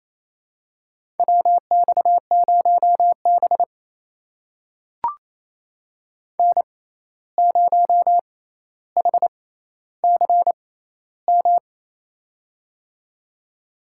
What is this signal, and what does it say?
Morse code, heard as a keyed tone, about 28 wpm, spaced as for 13 wpm, 700 Hz. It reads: WX0B N0HCM
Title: USA Call Signs - 6x Character Spacing - 28wpm